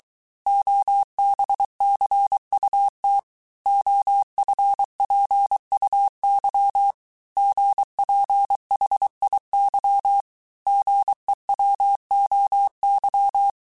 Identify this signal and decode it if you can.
Morse code, heard as a keyed tone, about 23 wpm, 800 Hz. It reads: OBCUT OFPUY GPHIY GEWOY
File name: cours-du-mercredi / audios / cours-855-texte-chiffre-vitesse-22-mpm